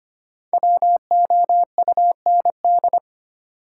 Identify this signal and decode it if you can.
Morse code, heard as a keyed tone, about 25 wpm, 700 Hz. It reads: WOUND
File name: 5 Letter Words - First 3 Letters Given - 25wpm